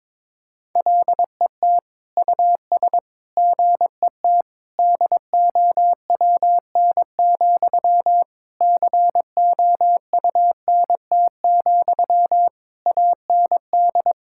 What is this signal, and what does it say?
Morse code, heard as a keyed tone, about 22 wpm, 700 Hz. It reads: LET US GET DOWN, COUNT, AND
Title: Sentences from Top 1600 Words 22wpm